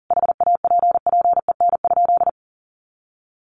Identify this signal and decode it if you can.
Morse code, heard as a keyed tone, about 40 wpm, 700 Hz. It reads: HAPPEN?
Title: Sentences from Top 400 Words 40wpm